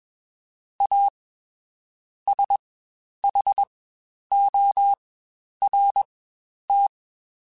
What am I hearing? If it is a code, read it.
Morse code, heard as a keyed tone, about 21 wpm, spaced as for 10 wpm, 800 Hz. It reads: A SHORT
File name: Quotes Spaced / Quotes Spaced-mp3s / Temper sp @20